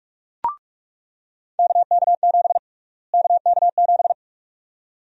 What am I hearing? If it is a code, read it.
Morse code, heard as a keyed tone, about 46 wpm, 700 Hz. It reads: KK7 KK7
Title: USA Call Sign Prefixes - Repeated 2x - 45wpm